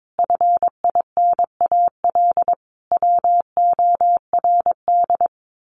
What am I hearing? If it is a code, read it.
Morse code, heard as a keyed tone, about 22 wpm, 700 Hz. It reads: FINAL WORD
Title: Sentences from Top 1000 Words 22wpm